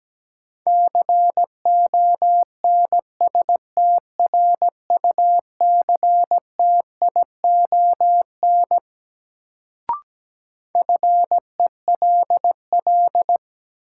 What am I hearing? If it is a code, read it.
Morse code, heard as a keyed tone, about 17 wpm, 700 Hz. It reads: CONSTRUCTION FELL
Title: Sentences from Top 1600 Words - New Word Drill - 17wpm